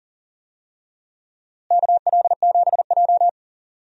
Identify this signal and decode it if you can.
Morse code, heard as a keyed tone, about 40 wpm, 700 Hz. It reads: KL7J
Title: DXCC-100 Call Signs - 4 Characters - Repeated 3x - 40wpm